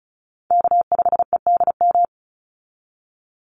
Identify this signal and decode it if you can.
Morse code, heard as a keyed tone, about 34 wpm, 700 Hz. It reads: K5EDM
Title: Call Signs Repeated 3x - 35wpm